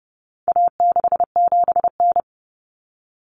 Morse code, heard as a keyed tone, about 30 wpm, 700 Hz. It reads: A67N
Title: Common Cut Numbers - 4-Digit 30wpm